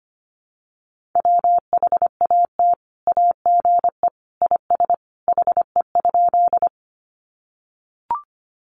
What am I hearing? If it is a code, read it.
Morse code, heard as a keyed tone, about 25 wpm, 700 Hz. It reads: WHAT AGE IS HE?
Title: Sentences from Top 500 Words 25wpm